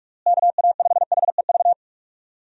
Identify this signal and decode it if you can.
Morse code, heard as a keyed tone, about 46 wpm, 700 Hz. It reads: KA5HEV
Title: USA Call Signs - 6 Characters - Repeated 3x - 45wpm